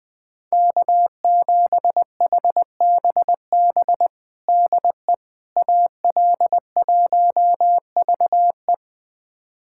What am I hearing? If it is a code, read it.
Morse code, heard as a keyed tone, about 20 wpm, 700 Hz. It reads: K7HBB DE AL1VE
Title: USA Callsign DE Callsign - 5 Characters - 20wpm